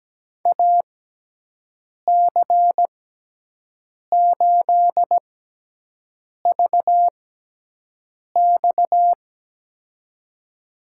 Morse code, heard as a keyed tone, about 17 wpm, spaced as for 6 wpm, 700 Hz. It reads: AC8VX
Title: USA Call Signs - 6x Character Spacing - 17wpm